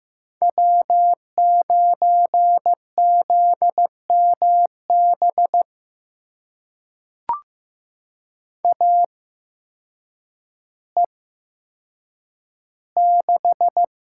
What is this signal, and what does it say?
Morse code, heard as a keyed tone, about 15 wpm, spaced as for 8 wpm, 700 Hz. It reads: W9ZMB AE6
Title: USA Call Signs - 5 Characters - 8x Character Spacing - 15wpm